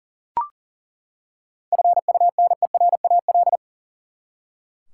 Morse code, heard as a keyed tone, about 40 wpm, 700 Hz. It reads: FUNERAL
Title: Seven Letter Words - 40wpm